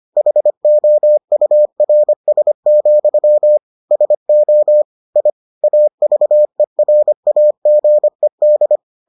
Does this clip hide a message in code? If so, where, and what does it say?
Yes: Morse code from 0.2 to 8.8 s, heard as a keyed tone, about 25 wpm, 600 Hz. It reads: HOURS, SO I AVERAGED